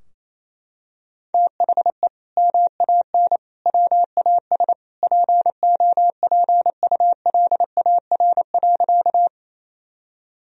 Morse code, heard as a keyed tone, about 28 wpm, 700 Hz. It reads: THE MAN WAS POPULAR.
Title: Sentences from Top 1000 Words - Review Part 3 - 28wpm